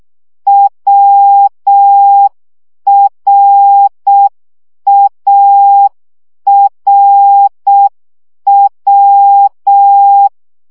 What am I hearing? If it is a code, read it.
Morse code, heard as a keyed tone, about 6 wpm, 800 Hz. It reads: WRARW
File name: CW gyakorlas a r w 5 baud